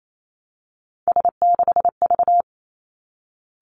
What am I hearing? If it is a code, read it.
Morse code, heard as a keyed tone, about 28 wpm, 700 Hz. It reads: S6V
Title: Three Challenging Letter-Number Combinations 28wpm